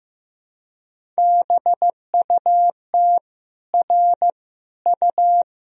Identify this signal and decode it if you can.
Morse code, heard as a keyed tone, about 15 wpm, 700 Hz. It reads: BUT R U
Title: Sentences from Top 300 Words and Abbreviations 15wpm